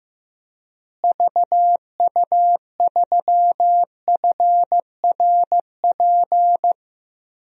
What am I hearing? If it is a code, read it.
Morse code, heard as a keyed tone, about 15 wpm, 700 Hz. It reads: VU3FRP
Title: DXCC-100 Call Signs - 6 Characters - Repeated 3x - 15wpm